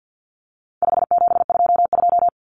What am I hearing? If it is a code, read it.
Morse code, heard as a keyed tone, about 50 wpm, 700 Hz. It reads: 5722
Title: Common Cut Numbers - 4-Digit 50wpm